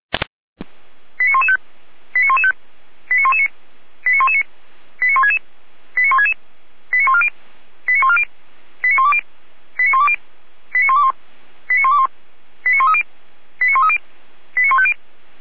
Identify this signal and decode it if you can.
Five-tone selective call (ZVEI), heard as quick runs of five tones, tones of about 70 ms. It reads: 89115 89115 89119 89119 89166 89166 89140 89140 89120 89120 89121 89121 89130 89130 89150
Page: audio.f85360.feuerwehr-orlamuende.org